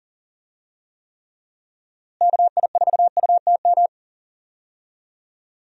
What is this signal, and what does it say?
Morse code, heard as a keyed tone, about 40 wpm, 700 Hz. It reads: KI4UTM
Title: USA Call Signs - 6 Characters - Repeated 3x - 40wpm